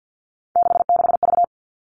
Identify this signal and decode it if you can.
Morse code, heard as a keyed tone, about 50 wpm, 700 Hz. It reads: 66V
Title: All Cut Numbers - 3-Digit 50wpm